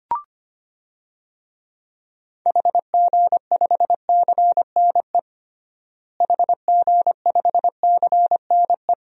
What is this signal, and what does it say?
Morse code, heard as a keyed tone, about 25 wpm, 700 Hz. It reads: HG5CNE HG5CNE
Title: DXCC-100 Call Signs - 6 Characters - Repeated 2x - 25wpm